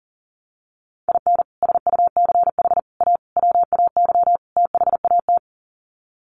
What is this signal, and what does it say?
Morse code, heard as a keyed tone, about 40 wpm, 700 Hz. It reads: IN SUCH A WAY THAT
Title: Sets of 5 Words 40wpm